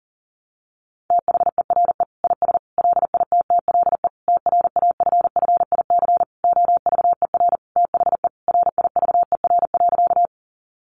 Morse code, heard as a keyed tone, about 40 wpm, 700 Hz. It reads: THERE IS LITTLE TRAFFIC OVER THE RIVER.
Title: Sentences from Top 1300 Words 40wpm